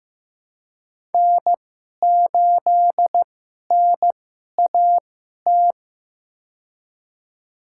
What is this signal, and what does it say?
Morse code, heard as a keyed tone, about 15 wpm, spaced as for 11 wpm, 700 Hz. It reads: N8NAT
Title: USA Call Signs - 5 Characters - 2x Character Spacing - 15wpm